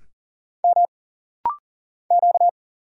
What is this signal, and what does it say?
Morse code, heard as a keyed tone, about 40 wpm, 700 Hz. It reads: M Q
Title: Single Letter-Number Alt - Rapid-Fire 40wpm